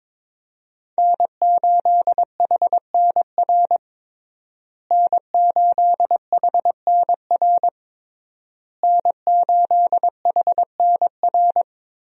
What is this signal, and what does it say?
Morse code, heard as a keyed tone, about 22 wpm, 700 Hz. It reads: N8HNR N8HNR N8HNR
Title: USA Call Signs - 5 Characters - Repeated 3x - 22wpm